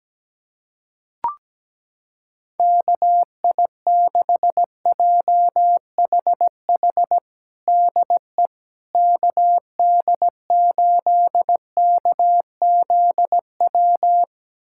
Morse code, heard as a keyed tone, about 17 wpm, 700 Hz. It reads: KI6JHH DE KD8KZW
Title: USA Callsign DE Callsign 17wpm